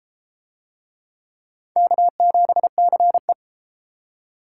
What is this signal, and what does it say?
Morse code, heard as a keyed tone, about 31 wpm, 700 Hz. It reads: K7CE